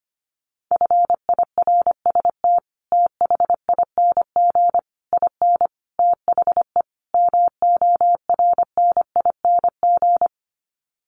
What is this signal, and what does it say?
Morse code, heard as a keyed tone, about 25 wpm, 700 Hz. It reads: FIRST THING IN THE MORNING
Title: Sets of 5 Words Encore2 25wpm